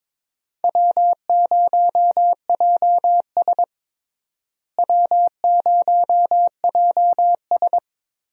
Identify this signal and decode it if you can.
Morse code, heard as a keyed tone, about 22 wpm, 700 Hz. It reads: W0JS W0JS
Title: USA Call Signs - 4 Characters - Repeated 3x - 22wpm